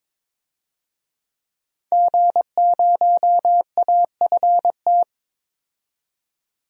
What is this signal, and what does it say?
Morse code, heard as a keyed tone, about 23 wpm, 700 Hz. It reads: G0AFT
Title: DXCC-100 Call Signs - 5 Characters - 22wpm